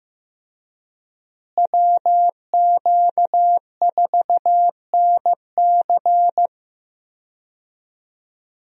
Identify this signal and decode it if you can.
Morse code, heard as a keyed tone, about 15 wpm, 700 Hz. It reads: WQ4NC
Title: USA Call Signs - 5 Characters - Repeated 3x - 15wpm